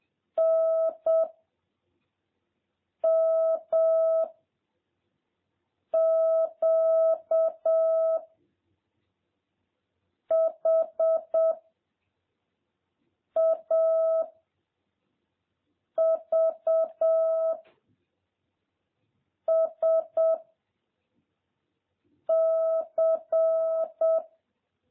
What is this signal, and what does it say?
Morse code, heard as a keyed tone, about 7 wpm, 650 Hz. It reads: N M Q H A V S C